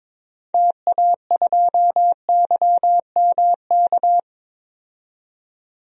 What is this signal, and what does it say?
Morse code, heard as a keyed tone, about 22 wpm, 700 Hz. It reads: TA2YMK
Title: DXCC-100 Call Signs - 6 Characters - 22wpm